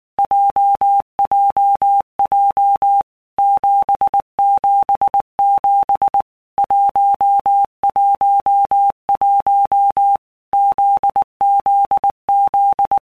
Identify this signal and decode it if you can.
Morse code, heard as a keyed tone, about 19 wpm, 800 Hz. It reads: JJJ 777 111 ZZZ
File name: All Groups 1-minute Blocks / Groups@18 / JZ17